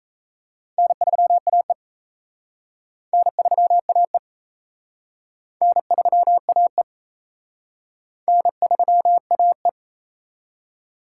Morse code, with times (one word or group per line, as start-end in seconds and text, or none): 0.8-1.7 N3AE
3.1-4.2 N3AE
5.6-6.8 N3AE
8.3-9.7 N3AE